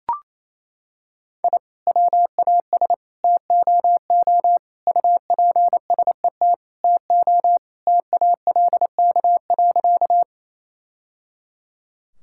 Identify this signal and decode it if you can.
Morse code, heard as a keyed tone, about 27 wpm, 700 Hz. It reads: I WAS TOO UPSET TO TALK.
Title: Sentences from Top 1000 Words - Review Part 5 - 28wpm